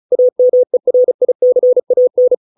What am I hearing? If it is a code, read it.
Morse code, heard as a keyed tone, about 34 wpm, 500 Hz. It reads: AMERICAN